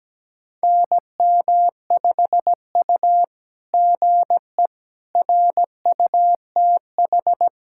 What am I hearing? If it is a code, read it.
Morse code, heard as a keyed tone, about 17 wpm, 700 Hz. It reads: NM5U GE RUTH
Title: SOTA QSO Practice 17wpm